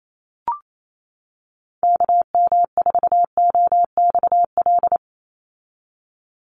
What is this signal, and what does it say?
Morse code, heard as a keyed tone, about 28 wpm, 700 Hz. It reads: KM4OXL
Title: USA Call Signs 28wpm